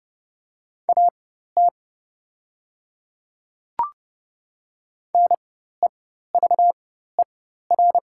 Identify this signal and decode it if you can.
Morse code, heard as a keyed tone, about 30 wpm, spaced as for 14 wpm, 700 Hz. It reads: AT NEVER
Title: 4x Character Spacing - Top 300 Words - 30wpm